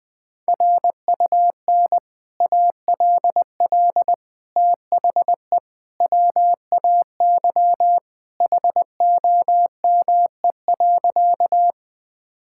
Morse code, heard as a keyed tone, about 20 wpm, 700 Hz. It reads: RUN ALL THE WAY HOME.
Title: Sentences with 3-Word Phrases - Spoken Beginning - 20wpm